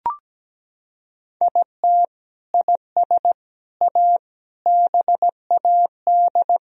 Morse code, heard as a keyed tone, about 17 wpm, 700 Hz. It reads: IT IS A BAD
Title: Sentences from Top 1000 Words - Review Part 2 - 17wpm